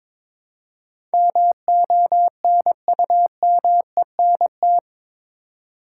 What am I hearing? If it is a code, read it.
Morse code, heard as a keyed tone, about 22 wpm, 700 Hz. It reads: MONUMENT
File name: practice / 22wpm / Eight Letter Words - 22wpm